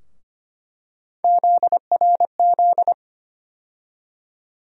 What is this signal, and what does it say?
Morse code, heard as a keyed tone, about 25 wpm, 700 Hz. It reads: ZRZ
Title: USA Call Sign Postfixes - 3 Letters 25wpm